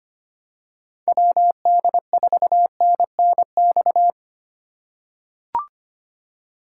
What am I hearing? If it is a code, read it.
Morse code, heard as a keyed tone, about 25 wpm, 700 Hz. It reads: WD4NNX
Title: USA Call Signs - 6 Characters - 4x Character Spacing - 25wpm